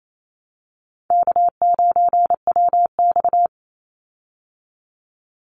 Morse code, heard as a keyed tone, about 28 wpm, 700 Hz. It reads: K9WX
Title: Easy Call Signs Repeated 3x - 28wpm